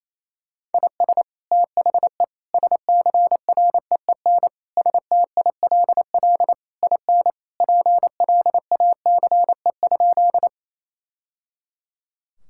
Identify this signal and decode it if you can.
Morse code, heard as a keyed tone, about 28 wpm, 700 Hz. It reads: IS THE SCREEN STILL IN PLACE?
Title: Sentences from Top 1000 Words - Review Part 6 - 28wpm